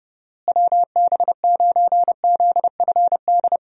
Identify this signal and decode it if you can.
Morse code, heard as a keyed tone, about 30 wpm, 700 Hz. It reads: WB9ZFD
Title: USA Call Signs - 6 Characters - Repeated 2x - 30wpm